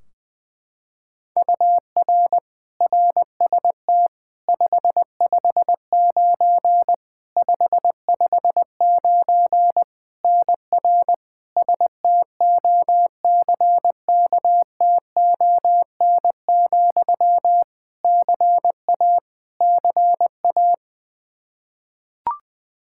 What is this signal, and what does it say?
Morse code, heard as a keyed tone, about 21 wpm, 700 Hz. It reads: UR RST 559 559 NR STOCKTON, CA CA
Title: Practice QSO - Exchange 1 - Beginning - 20wpm